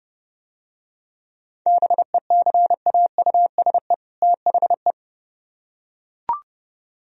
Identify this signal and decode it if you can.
Morse code, heard as a keyed tone, about 29 wpm, 700 Hz. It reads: BECAUSE THE